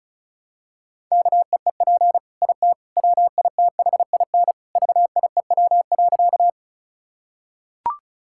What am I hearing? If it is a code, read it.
Morse code, heard as a keyed tone, about 34 wpm, 700 Hz. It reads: KEEP IT WITHIN VIEW.
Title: Sentences from Top 1000 Words - Review Part 6 - 35wpm